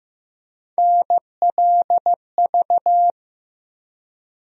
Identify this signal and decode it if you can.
Morse code, heard as a keyed tone, about 15 wpm, 700 Hz. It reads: NLV
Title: USA Call Sign Postfixes - 3 Letters 15wpm